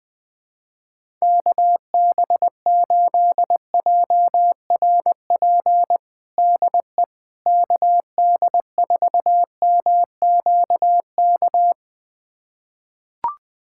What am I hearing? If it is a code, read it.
Morse code, heard as a keyed tone, about 20 wpm, 700 Hz. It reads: KB8JRP DE KD4MQK
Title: USA Callsign DE Callsign - 6 Characters - 20wpm